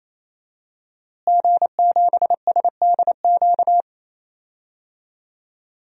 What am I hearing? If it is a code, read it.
Morse code, heard as a keyed tone, about 28 wpm, 700 Hz. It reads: G7SDQ
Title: DXCC-100 Call Signs - 5 Characters - Repeated 3x - 28wpm